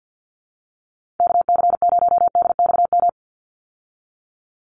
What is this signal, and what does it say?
Morse code, heard as a keyed tone, about 50 wpm, 700 Hz. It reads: KC0DXM
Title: USA Call Signs 50wpm